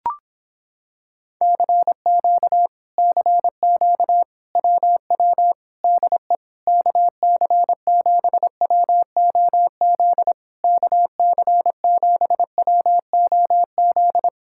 Morse code, heard as a keyed tone, about 26 wpm, 700 Hz. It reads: CQ CQ WW DE KC7WOZ KC7WOZ